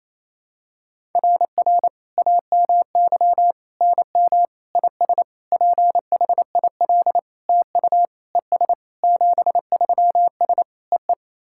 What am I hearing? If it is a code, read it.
Morse code, heard as a keyed tone, about 28 wpm, 700 Hz. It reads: RR AMY NM IS PHIL TU ES 73S EE
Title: SOTA QSO Practice 28wpm